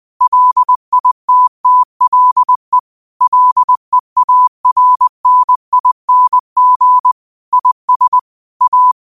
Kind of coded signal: Morse code